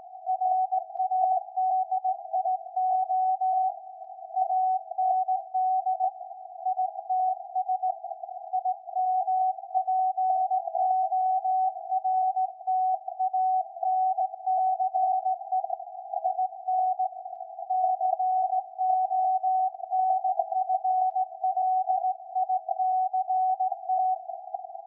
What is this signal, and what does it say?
Morse code, heard as a keyed tone, about 17 wpm, 750 Hz. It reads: RADIO AND ITS IMPORTANCE IN CONFLICT
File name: iz7slz / VLF / SAQ 13-02-2023